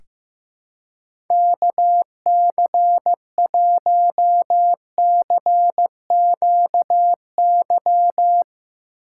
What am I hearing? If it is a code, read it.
Morse code, heard as a keyed tone, about 15 wpm, 700 Hz. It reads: KC1CQY
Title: USA Call Signs - 6x Character Spacing - 15wpm